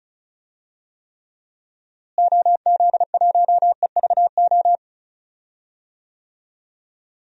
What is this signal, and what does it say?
Morse code, heard as a keyed tone, about 36 wpm, 700 Hz. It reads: OZ1EVO
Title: DXCC-100 Call Signs - 6 Characters - Repeated 2x - 35wpm